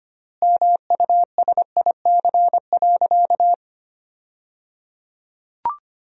Morse code, heard as a keyed tone, about 25 wpm, 700 Hz. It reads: MUSIC.
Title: 3x Word-Spacing - Sentences from Top 300 Words - 25wpm